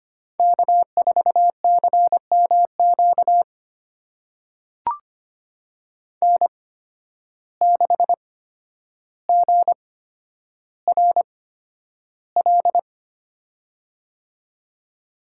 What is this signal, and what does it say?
Morse code, heard as a keyed tone, about 25 wpm, spaced as for 11 wpm, 700 Hz. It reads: K4CMQ N6GRL
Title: USA Call Signs - 8x Character Spacing - 25wpm